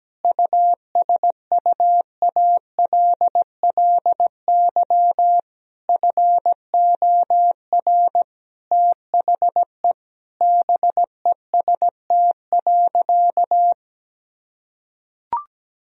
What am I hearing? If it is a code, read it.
Morse code, heard as a keyed tone, about 17 wpm, 700 Hz. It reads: USUALLY FOR THE BEST.
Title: Sentences from Top 1000 Words - Review Part 3 - 17wpm